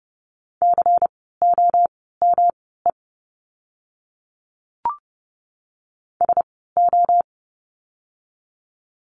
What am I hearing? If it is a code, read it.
Morse code, heard as a keyed tone, about 30 wpm, spaced as for 17 wpm, 700 Hz. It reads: COME SO
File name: course / 30:10wpm / 230 Lesson 37f - Letter x - Words 30wpm